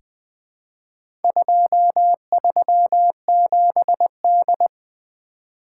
Morse code, heard as a keyed tone, about 20 wpm, 700 Hz. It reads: 237D
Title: Four Letter-Number Combinations - 20wpm